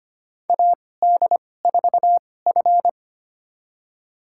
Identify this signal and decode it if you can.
Morse code, heard as a keyed tone, about 25 wpm, spaced as for 18 wpm, 700 Hz. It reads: AD4F
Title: USA Call Signs - 4 Characters - 2x Character Spacing - 25wpm